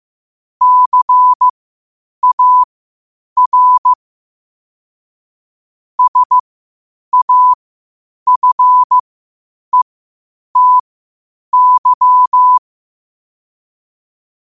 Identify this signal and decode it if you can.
Morse code, heard as a keyed tone, about 15 wpm, spaced as for 8 wpm, 1000 Hz. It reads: CAR SAFETY
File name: qotdfiles / 8wpm